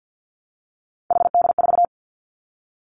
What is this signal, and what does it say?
Morse code, heard as a keyed tone, about 50 wpm, 700 Hz. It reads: HD4
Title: Three Challenging Letter-Number Combinations 50wpm